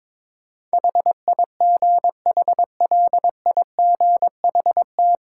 Morse code, heard as a keyed tone, about 21 wpm, 690 Hz. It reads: HIGHLIGHT
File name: practice / 22wpm / Easy Sentences with Hints - 22wpm